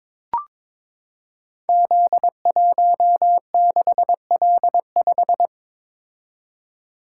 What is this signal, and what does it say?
Morse code, heard as a keyed tone, about 22 wpm, 700 Hz. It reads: Z16L5
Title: Code Groups - 5 Letter:Numbers - 22wpm